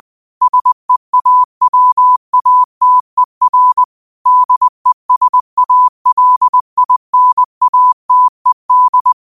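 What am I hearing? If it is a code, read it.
Morse code, heard as a keyed tone, about 20 wpm, 1000 Hz. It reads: SEAWATER DESALINATED
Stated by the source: Morse code practice at 20 wpm: famous quotations sent in telegraphy, from URVAG.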